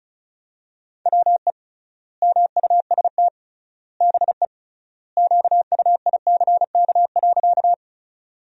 Morse code, heard as a keyed tone, about 34 wpm, 700 Hz. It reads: WE MUST BE QUICK.
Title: 3x Word-Spacing - Sentences from Top 500 Words - 35wpm